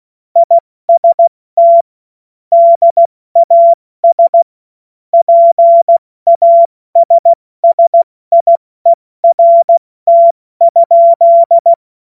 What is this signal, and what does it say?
Morse code, heard as a keyed tone, about 16 wpm, 680 Hz. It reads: IST DAS PASSIERT?
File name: Witz-09-September-2025-70-F80